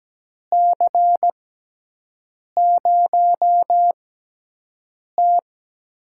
Morse code, heard as a keyed tone, about 17 wpm, spaced as for 6 wpm, 700 Hz. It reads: C0T
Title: USA Call Signs - 6x Character Spacing - 17wpm